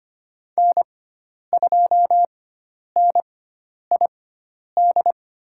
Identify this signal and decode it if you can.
Morse code, heard as a keyed tone, about 25 wpm, spaced as for 10 wpm, 700 Hz. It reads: N2NID
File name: Lesson 16g - Letter c - Call Signs 25wpm